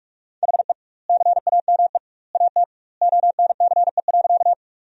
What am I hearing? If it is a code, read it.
Morse code, heard as a keyed tone, about 46 wpm, 700 Hz. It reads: HE CAME AT ONCE.
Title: 2x Word-Spacing - Sentences from Top 300 Words - 45wpm